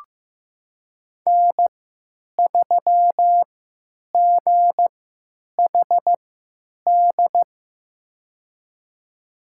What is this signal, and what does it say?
Morse code, heard as a keyed tone, about 15 wpm, spaced as for 9 wpm, 700 Hz. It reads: N3GHD
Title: USA Call Signs - 5 Characters - 3x Character Spacing - 15wpm